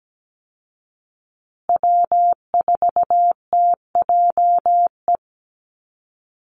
Morse code, heard as a keyed tone, about 17 wpm, 700 Hz. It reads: W4TJE
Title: Call Signs Repeated 2x - 17wpm